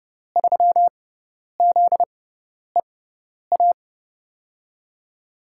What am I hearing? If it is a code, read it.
Morse code, heard as a keyed tone, about 30 wpm, spaced as for 10 wpm, 700 Hz. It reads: 3ZEA
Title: USA Call Signs - 6 Characters - 6x Character Spacing - 30wpm